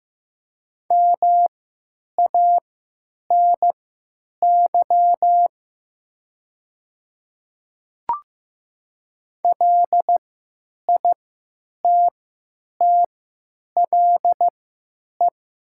Morse code, heard as a keyed tone, about 15 wpm, spaced as for 9 wpm, 700 Hz. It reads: MANY LITTLE
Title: 3x Character Spacing - Top 200 Words - 15wpm